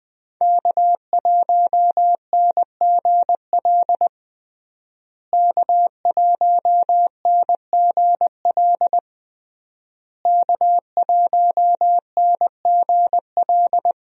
Morse code, heard as a keyed tone, about 20 wpm, 700 Hz. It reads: K1NGL K1NGL K1NGL
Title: USA Call Signs - 5 Characters - Repeated 3x - 20wpm